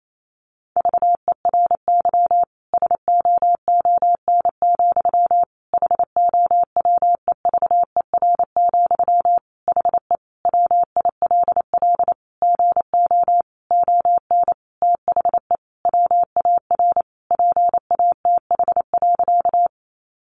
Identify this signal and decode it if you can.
Morse code, heard as a keyed tone, about 28 wpm, 700 Hz. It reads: VERY SOON, HOWEVER, HE WILL GO ON THE WAR PATH.
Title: Sentences from Top 1400 Words 28wpm